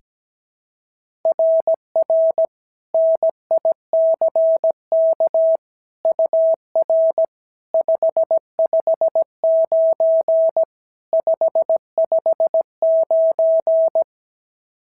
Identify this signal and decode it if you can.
Morse code, heard as a keyed tone, about 16 wpm, 650 Hz. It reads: RR NICK UR 559 559
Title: SOTA QSO Practice 17wpm